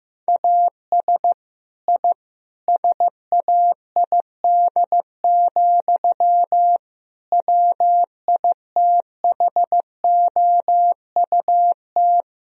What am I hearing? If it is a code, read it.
Morse code, heard as a keyed tone, about 15 wpm, 700 Hz. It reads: AS I SAID, WITHOUT
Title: Sentences from Top 1000 Words - Review Part 3 - 15wpm